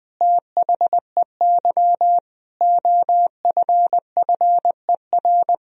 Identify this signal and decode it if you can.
Morse code, heard as a keyed tone, about 21 wpm, 700 Hz. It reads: THEY OFFER
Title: Sentences from Top 1000 Words - Review Part 4 - 20wpm